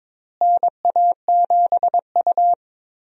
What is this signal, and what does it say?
Morse code, heard as a keyed tone, about 22 wpm, 700 Hz. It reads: NA7U
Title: USA Call Signs - 4 Characters - Repeated 2x - 22wpm